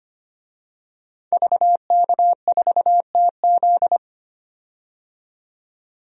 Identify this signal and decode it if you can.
Morse code, heard as a keyed tone, about 25 wpm, 700 Hz. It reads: VK4TZ
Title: DXCC-100 Call Signs - 5 Characters - 25wpm